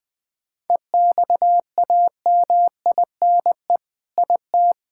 Morse code, heard as a keyed tone, about 20 wpm, 700 Hz. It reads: EXAMINE IT
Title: Sentences from Top 1600 Words 20wpm